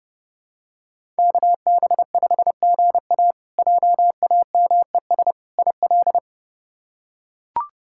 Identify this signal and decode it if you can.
Morse code, heard as a keyed tone, about 32 wpm, 700 Hz. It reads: KB5GA JAMES IL
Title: K1USN SST QSO Practice - 30wpm